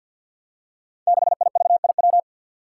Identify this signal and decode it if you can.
Morse code, heard as a keyed tone, about 52 wpm, 700 Hz. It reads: 6IVIW
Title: Five Challenging Letter-Number Combinations 50wpm